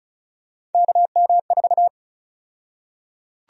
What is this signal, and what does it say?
Morse code, heard as a keyed tone, about 36 wpm, 700 Hz. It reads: KM4
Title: USA Call Sign Prefixes - 35wpm